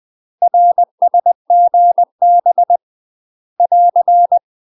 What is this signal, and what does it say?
Morse code, heard as a keyed tone, about 20 wpm, 700 Hz. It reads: RSGB <AR>
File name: UK MASTER QSOs TRAINING / 20WPM / UK MASTER QSOs 0072